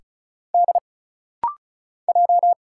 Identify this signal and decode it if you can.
Morse code, heard as a keyed tone, about 34 wpm, 700 Hz. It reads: D J
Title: Single Letters - Rapid-Fire 35wpm